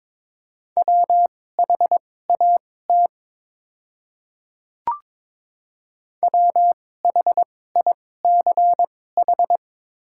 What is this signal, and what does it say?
Morse code, heard as a keyed tone, about 22 wpm, spaced as for 16 wpm, 700 Hz. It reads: WHAT WHICH
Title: 2x Character Spacing - Top 100 Words - 22wpm